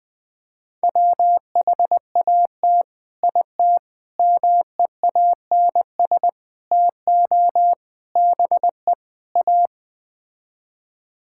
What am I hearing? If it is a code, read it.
Morse code, heard as a keyed tone, about 20 wpm, 700 Hz. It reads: WHAT IT MEANS TO BE A
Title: Sets of 6 Words - 20wpm